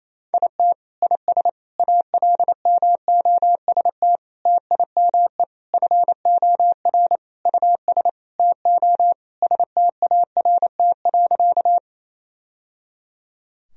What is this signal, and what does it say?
Morse code, heard as a keyed tone, about 28 wpm, 700 Hz. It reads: IT IS ALMOST TIME FOR US TO START.
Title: Sentences from Top 1000 Words - Review Part 4 - 28wpm